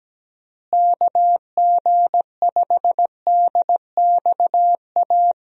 Morse code, heard as a keyed tone, about 17 wpm, 700 Hz. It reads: KG5DXA
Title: USA Call Signs - 6 Characters - Repeated 2x - 17wpm